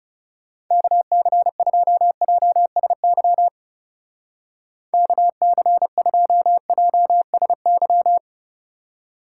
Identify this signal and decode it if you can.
Morse code, heard as a keyed tone, about 39 wpm, 700 Hz. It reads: KC2JSY KC2JSY